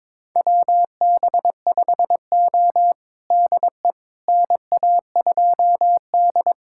Morse code, heard as a keyed tone, about 22 wpm, 700 Hz. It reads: WB5O DE NA2D
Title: USA Callsign DE Callsign - 4 Characters - 22wpm